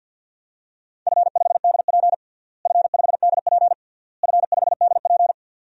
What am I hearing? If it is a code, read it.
Morse code, heard as a keyed tone, about 52 wpm, 700 Hz. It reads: F5DP F5DP F5DP
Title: DXCC-100 Call Signs - 4 Characters - Repeated 3x - 50wpm